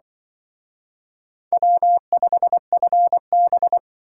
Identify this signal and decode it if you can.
Morse code, heard as a keyed tone, about 25 wpm, 700 Hz. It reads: W5FB
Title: Speed Racing - USA 1x2 Call Signs 20wpm